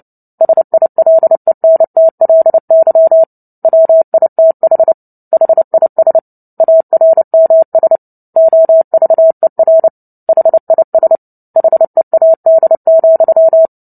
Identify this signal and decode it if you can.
Morse code, heard as a keyed tone, about 30 wpm, 650 Hz. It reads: SILENTLY WITH HIS ARMS OVER HIS HEAD,
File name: Book Two / 03-Chapter 3 - 29 WPM